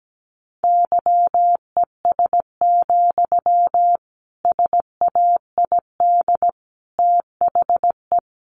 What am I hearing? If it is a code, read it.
Morse code, heard as a keyed tone, about 17 wpm, 700 Hz. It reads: YES, SAID THE
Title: A Sound Of Thunder 17wpm